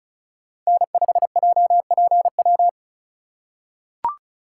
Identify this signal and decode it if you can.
Morse code, heard as a keyed tone, about 36 wpm, 700 Hz. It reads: N5JPW
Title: USA Call Signs - 5 Characters - 3x Character Spacing - 35wpm